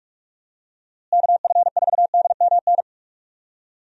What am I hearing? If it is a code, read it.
Morse code, heard as a keyed tone, about 44 wpm, 700 Hz. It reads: KF4DMN
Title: USA Call Signs - 6 Characters - Repeated 3x - 45wpm